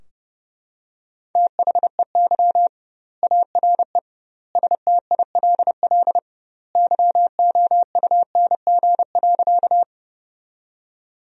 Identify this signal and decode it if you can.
Morse code, heard as a keyed tone, about 32 wpm, 700 Hz. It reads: THEY ARE STILL YOUNG.
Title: 2x Word-Spacing - Sentences from Top 400 Words - 30wpm